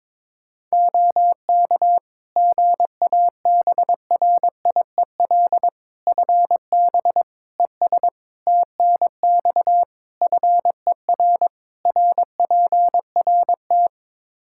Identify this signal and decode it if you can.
Morse code, heard as a keyed tone, about 22 wpm, 700 Hz. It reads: OK GABRIEL FB ES TNX FER RPRT
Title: Practice QSO - Exchange 2 - Info - 22wpm